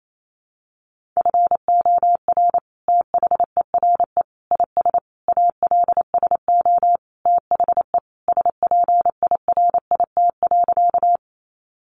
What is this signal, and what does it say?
Morse code, heard as a keyed tone, about 28 wpm, 700 Hz. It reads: FOR THERE IS ALSO THE SPIRIT.
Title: Sentences from Top 1400 Words 28wpm